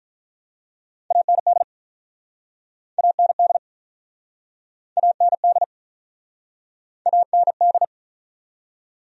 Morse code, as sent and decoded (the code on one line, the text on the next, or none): .- -. -.. / .- -. -.. / .- -. -.. / .- -. -..
AND AND AND AND